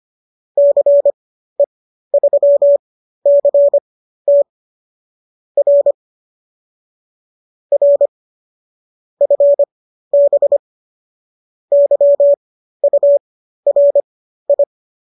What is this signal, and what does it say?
Morse code, heard as a keyed tone, about 25 wpm, spaced as for 13 wpm, 570 Hz. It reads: CE3CT R R FB YURI